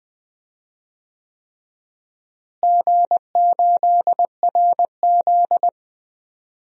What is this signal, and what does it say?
Morse code, heard as a keyed tone, about 20 wpm, 700 Hz. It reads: G8RZ
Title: DXCC-100 Call Signs - 4 Characters - Repeated 3x - 20wpm